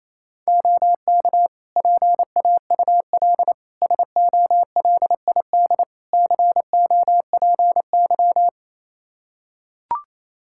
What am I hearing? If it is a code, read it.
Morse code, heard as a keyed tone, about 27 wpm, 700 Hz. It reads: OK PAUL SOLID COPY
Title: K1USN SST QSO Practice 28wpm